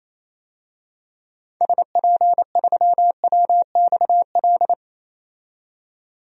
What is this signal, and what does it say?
Morse code, heard as a keyed tone, about 28 wpm, 700 Hz. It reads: SP3WXL
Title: DXCC-100 Call Signs - 6 Characters - 28wpm